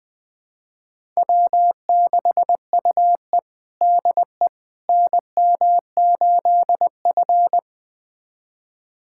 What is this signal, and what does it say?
Morse code, heard as a keyed tone, about 20 wpm, 700 Hz. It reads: W6UE DE NM8F
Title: USA Callsign DE Callsign - 4 Characters - 20wpm